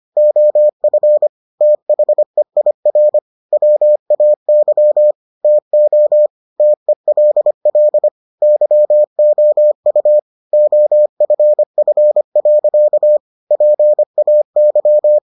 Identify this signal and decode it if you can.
Morse code, heard as a keyed tone, about 25 wpm, 600 Hz. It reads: OF THEIR WAY TO TELL YOU OFF. PAY